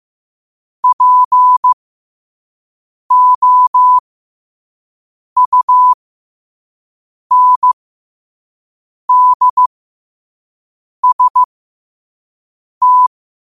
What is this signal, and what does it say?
Morse code, heard as a keyed tone, about 15 wpm, spaced as for 5 wpm, 1000 Hz. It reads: POUNDST
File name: qotdfiles / 5wpm